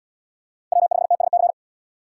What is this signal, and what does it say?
Morse code, heard as a keyed tone, about 77 wpm, 700 Hz. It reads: W5TSC